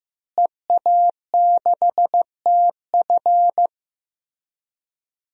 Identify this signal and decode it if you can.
Morse code, heard as a keyed tone, about 15 wpm, 700 Hz. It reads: EA6TF
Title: DXCC-100 Call Signs - 5 Characters - Repeated 3x - 15wpm